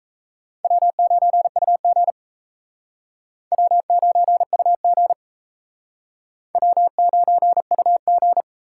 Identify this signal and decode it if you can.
Morse code, heard as a keyed tone, about 36 wpm, 700 Hz. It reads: W9UG W9UG W9UG